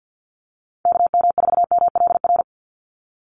Morse code, heard as a keyed tone, about 50 wpm, 700 Hz. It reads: KM4MLR
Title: USA Call Signs 50wpm